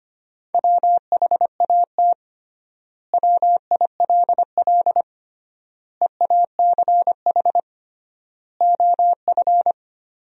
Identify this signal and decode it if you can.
Morse code, heard as a keyed tone, about 25 wpm, 700 Hz. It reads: WHAT WILL EACH OF
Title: 3x Word-Spacing - Sentences from Top 200 Words - 25wpm